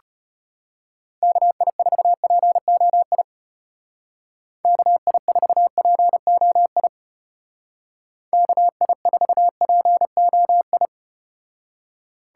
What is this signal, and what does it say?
Morse code, heard as a keyed tone, about 32 wpm, 700 Hz. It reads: KI4POI KI4POI KI4POI